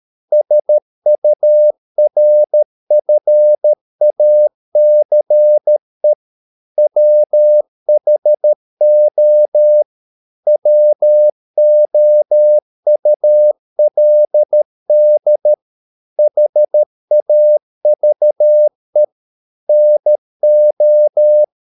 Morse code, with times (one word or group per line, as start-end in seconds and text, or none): 0.3-6.1 SURFACE
6.8-9.8 WHO
10.5-15.5 WOULD
16.2-19.0 HAVE
19.7-21.4 NO